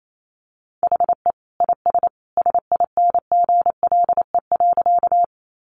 Morse code, heard as a keyed tone, about 28 wpm, 700 Hz. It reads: HE IS SINGLE.
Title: Sentences from Top 700 Words 28wpm